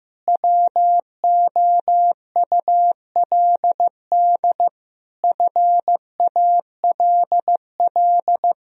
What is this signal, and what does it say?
Morse code, heard as a keyed tone, about 15 wpm, 700 Hz. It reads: WOULD FALL
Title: Sentences from Top 1000 Words - Review Part 2 - 15wpm